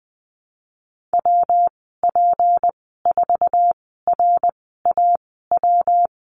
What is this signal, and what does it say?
Morse code, heard as a keyed tone, about 20 wpm, spaced as for 14 wpm, 700 Hz. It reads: WP4RAW
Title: 179 Lesson 29g - Slash - Call Signs 20wpm